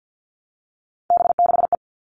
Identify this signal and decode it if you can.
Morse code, heard as a keyed tone, about 50 wpm, 700 Hz. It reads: B6E
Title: All Cut Numbers - 3-Digit 50wpm